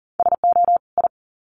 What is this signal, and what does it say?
Morse code, heard as a keyed tone, about 40 wpm, 700 Hz. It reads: SO I
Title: Sets of 2 Words 40wpm